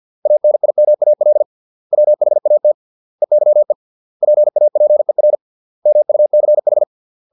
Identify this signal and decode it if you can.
Morse code, heard as a keyed tone, about 50 wpm, 600 Hz. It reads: ANIMAL WHAT EYE PAPER MUCH